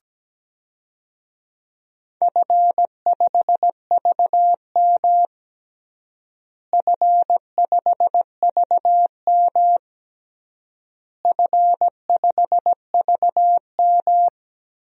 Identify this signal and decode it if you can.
Morse code, heard as a keyed tone, about 17 wpm, 700 Hz. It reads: F5VM F5VM F5VM
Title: DXCC-100 Call Signs - 4 Characters - Repeated 3x - 17wpm